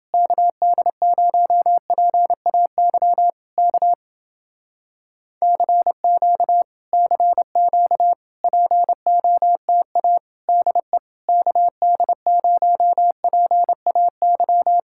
Morse code, heard as a keyed tone, about 29 wpm, 700 Hz. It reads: KD0PAY K CQ CQ POTA DE KD0PAY